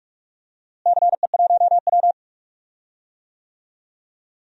Morse code, heard as a keyed tone, about 45 wpm, 700 Hz. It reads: CE1W
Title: DXCC-100 Call Signs - 4 Characters - Repeated 2x - 45wpm